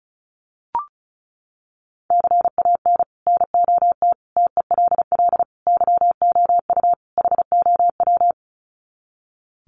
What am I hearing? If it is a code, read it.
Morse code, heard as a keyed tone, about 34 wpm, 700 Hz. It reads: CAN NOT TELL YOU HOW
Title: Sets of 5 Words Encore2 35wpm